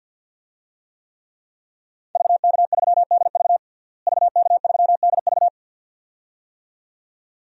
Morse code, heard as a keyed tone, about 46 wpm, 700 Hz. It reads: VK3DV VK3DV
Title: DXCC-100 Call Signs - 5 Characters - Repeated 2x - 50wpm